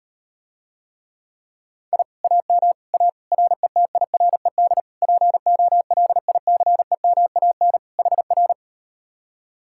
Morse code, heard as a keyed tone, about 38 wpm, 700 Hz. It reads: I AM A RETIRED POLICEMAN HR